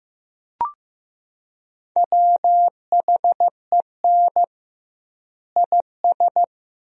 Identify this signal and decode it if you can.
Morse code, heard as a keyed tone, about 15 wpm, 700 Hz. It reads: WHEN IS
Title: 2x Word-Spacing - Sentences from Top 100 Words - 15wpm